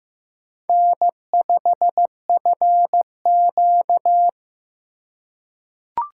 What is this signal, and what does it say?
Morse code, heard as a keyed tone, about 15 wpm, 700 Hz. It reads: N5FQ
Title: USA Call Signs - 4 Characters - 4x Character Spacing - 15wpm